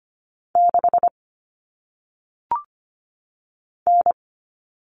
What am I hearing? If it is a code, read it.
Morse code, heard as a keyed tone, about 25 wpm, 700 Hz. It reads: 6 N
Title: Common Cut Numbers - 1-Digit 25wpm